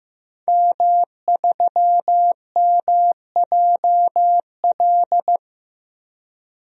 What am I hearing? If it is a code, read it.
Morse code, heard as a keyed tone, about 16 wpm, 700 Hz. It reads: M3MJL
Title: DXCC-100 Call Signs - 5 Characters - Repeated 2x - 15wpm